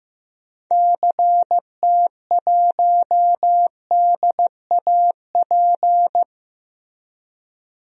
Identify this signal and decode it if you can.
Morse code, heard as a keyed tone, about 15 wpm, 700 Hz. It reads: CT1DAP
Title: DXCC-100 Call Signs - 6 Characters - Repeated 3x - 15wpm